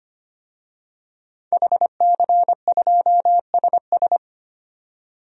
Morse code, heard as a keyed tone, about 25 wpm, 690 Hz. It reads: HC2SS
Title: DXCC-100 Call Signs - 5 Characters - 25wpm